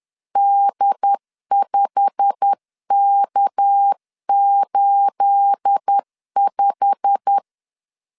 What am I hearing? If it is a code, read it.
Morse code, heard as a keyed tone, about 10 wpm, 800 Hz. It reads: D5K85
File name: LEZIONI / 9 28 10wpm 2842